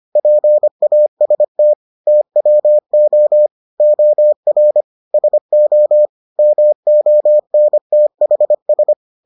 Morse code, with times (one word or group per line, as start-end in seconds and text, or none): 0.2-1.7 PAST
2.1-3.5 TWO
3.8-4.8 OR
5.1-6.1 SO
6.4-8.9 MONTHS